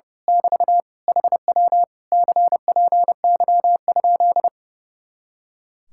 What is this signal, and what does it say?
Morse code, heard as a keyed tone, about 29 wpm, 700 Hz. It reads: <BT> HW CPY?
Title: Speed-Racing Common QSO Phrases Encore - 30wpm